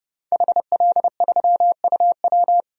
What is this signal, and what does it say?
Morse code, heard as a keyed tone, about 32 wpm, 700 Hz. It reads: HL3UW
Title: DXCC-100 Call Signs - 5 Characters - Repeated 2x - 30wpm